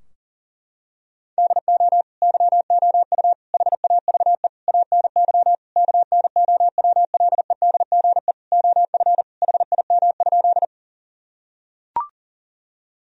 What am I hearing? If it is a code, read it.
Morse code, heard as a keyed tone, about 40 wpm, 700 Hz. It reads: DO YOU HAVE ANY KNOWLEDGE OF HIM?
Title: Sentences from Top 1000 Words - Review Part 3 - 40wpm